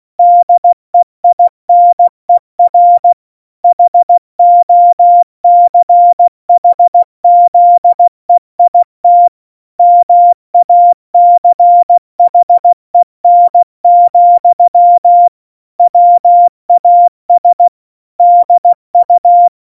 Morse code, heard as a keyed tone, about 16 wpm, 700 Hz. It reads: DEINER HOCHZEIT MACHEN, WAS DU
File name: Morsekurs / Witze / Witz 24. September 2019, 70 F80